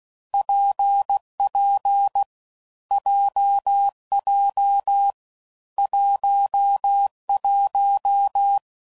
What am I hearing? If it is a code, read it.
Morse code, heard as a keyed tone, about 16 wpm, 800 Hz. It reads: PP JJ 11
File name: Combinations / Difficult Combos / WPJ WPJ1